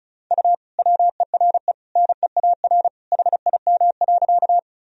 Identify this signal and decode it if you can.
Morse code, heard as a keyed tone, about 35 wpm, 700 Hz. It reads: U WERE NEAR HIM.
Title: Sentences from Top 200 Words and Abbreviations 35wpm